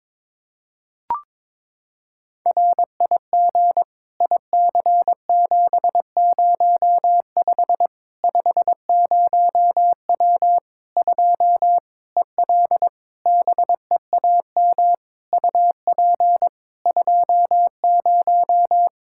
Morse code, heard as a keyed tone, about 22 wpm, 700 Hz. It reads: RIG IC705 50W 2 EL BEAM UP 20
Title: Practice QSO - Exchange 2 - Info - 22wpm